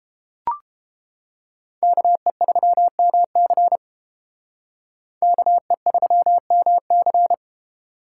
Morse code, heard as a keyed tone, about 30 wpm, 700 Hz. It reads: KE3MC KE3MC